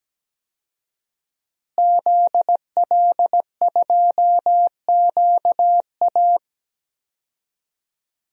Morse code, heard as a keyed tone, about 17 wpm, 700 Hz. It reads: ZL2QA
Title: DXCC-100 Call Signs - 5 Characters - Repeated 2x - 17wpm